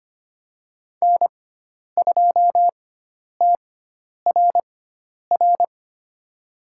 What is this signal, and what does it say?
Morse code, heard as a keyed tone, about 25 wpm, spaced as for 10 wpm, 700 Hz. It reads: N2TRR
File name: Lesson 16g - Letter c - Call Signs 25wpm